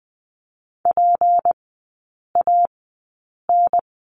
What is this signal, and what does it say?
Morse code, heard as a keyed tone, about 20 wpm, spaced as for 8 wpm, 690 Hz. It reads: PAN
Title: 191 Lesson 31e - Letter v - Three Characters 20wpm